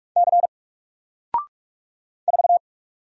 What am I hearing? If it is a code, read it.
Morse code, heard as a keyed tone, about 46 wpm, 700 Hz. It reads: C 4
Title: Single Letter-Number 45wpm